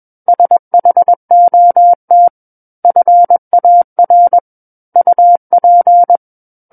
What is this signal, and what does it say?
Morse code, heard as a keyed tone, about 21 wpm, 700 Hz. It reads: SHOT FAR UP
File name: Book One / 12-Chapter 12 - 21 WPM